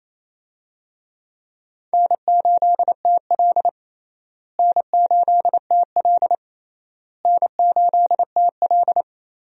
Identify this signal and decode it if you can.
Morse code, heard as a keyed tone, about 28 wpm, 700 Hz. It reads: N8TL N8TL N8TL
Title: USA Call Signs - 4 Characters - Repeated 3x - 28wpm